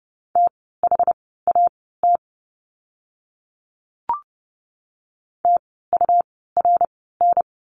Morse code, heard as a keyed tone, about 30 wpm, spaced as for 17 wpm, 700 Hz. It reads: THAT TURN
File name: course / 30:10wpm / 230 Lesson 37f - Letter x - Words 30wpm